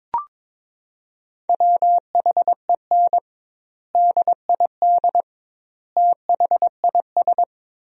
Morse code, heard as a keyed tone, about 22 wpm, 700 Hz. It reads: WHEN DID THIS
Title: 2x Word-Spacing - Sentences from Top 400 Words - 22wpm